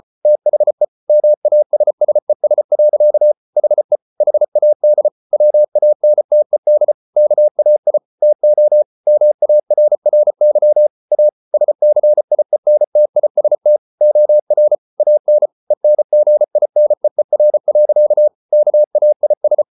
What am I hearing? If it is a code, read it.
Morse code, heard as a keyed tone, about 34 wpm, 600 Hz. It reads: THE MASSES. HE HAD WANTED KAI TO MARRY A SCIENTIST OR AN ENGINEER. KAIS